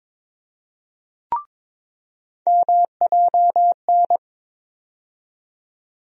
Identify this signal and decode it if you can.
Morse code, heard as a keyed tone, about 21 wpm, 700 Hz. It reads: MJN
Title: USA Call Sign Postfixes - 3 Letters 22wpm